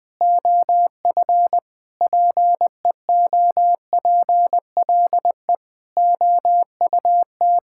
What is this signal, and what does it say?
Morse code, heard as a keyed tone, about 20 wpm, 700 Hz. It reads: OF PEOPLE OUT
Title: Sets of 9 Words - 20wpm